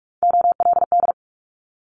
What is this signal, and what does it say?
Morse code, heard as a keyed tone, about 44 wpm, 700 Hz. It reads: OLD
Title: Multi-letter Phonemes Quick Fire 45wpm